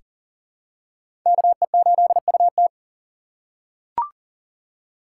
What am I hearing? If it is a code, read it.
Morse code, heard as a keyed tone, about 40 wpm, 700 Hz. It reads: KE8UT
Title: USA Call Signs - 4x Character Spacing - 40wpm